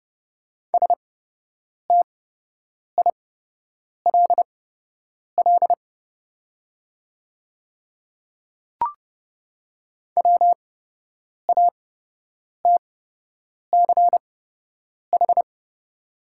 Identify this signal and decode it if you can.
Morse code, heard as a keyed tone, about 30 wpm, spaced as for 8 wpm, 700 Hz. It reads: STILL WATCH
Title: 8x Character Spacing - Top 300 Words - 30wpm